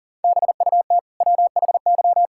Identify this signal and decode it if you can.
Morse code, heard as a keyed tone, about 37 wpm, 700 Hz. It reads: BUT WHY
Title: Sentences from Top 1000 Words - Review Part 6 - 40wpm